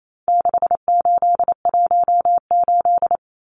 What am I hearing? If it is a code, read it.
Morse code, heard as a keyed tone, about 28 wpm, 700 Hz. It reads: 6818
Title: Numbers - 4-Digit 28wpm